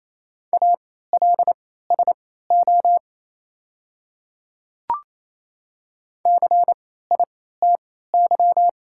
Morse code, heard as a keyed tone, about 28 wpm, spaced as for 16 wpm, 700 Hz. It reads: ALSO CITY
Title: 3x Character Spacing - Top 500 Words - 28wpm